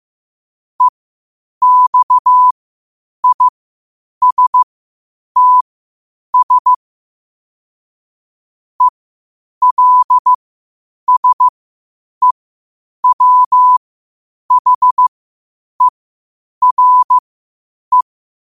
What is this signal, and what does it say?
Morse code, heard as a keyed tone, about 15 wpm, spaced as for 8 wpm, 1000 Hz. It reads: EXISTS ELSEWHERE
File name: qotdfiles / 8wpm